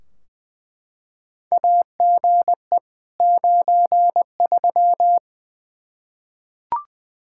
Morse code, heard as a keyed tone, about 20 wpm, 700 Hz. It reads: AGE 93
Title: K1USN SST QSO Practice 20wpm